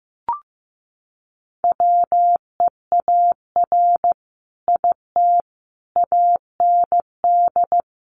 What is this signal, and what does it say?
Morse code, heard as a keyed tone, about 15 wpm, 700 Hz. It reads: WEAR IT AND
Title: Sentences from Top 1500 Words 15wpm